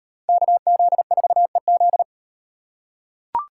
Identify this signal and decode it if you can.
Morse code, heard as a keyed tone, about 38 wpm, 700 Hz. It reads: KZ4EZ